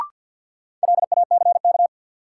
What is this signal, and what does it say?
Morse code, heard as a keyed tone, about 50 wpm, 700 Hz. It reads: LACK
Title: Sentences from Top 1100 Words - New Word Drill - 50wpm